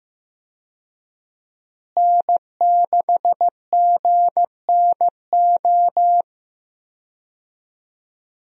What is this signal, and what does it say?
Morse code, heard as a keyed tone, about 15 wpm, 700 Hz. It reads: N6GNO
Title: USA Call Signs - 5 Characters - Repeated 3x - 15wpm